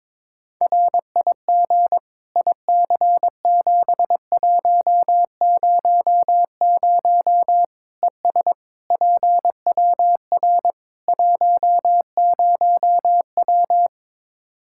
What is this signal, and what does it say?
Morse code, heard as a keyed tone, about 22 wpm, 700 Hz. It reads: RIG IC7100 ES PWR 10W
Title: K1USN SST QSO Practice 22wpm